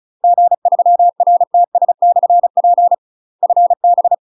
Morse code, heard as a keyed tone, about 34 wpm, 700 Hz. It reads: G3RTS/P FB
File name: UK MASTER QSOs TRAINING / 35WPM / UK MASTER QSOs 0025